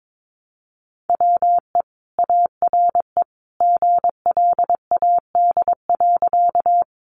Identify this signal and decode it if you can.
Morse code, heard as a keyed tone, about 22 wpm, 700 Hz. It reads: WE ARE GLAD.
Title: Sentences from Top 800 Words 22wpm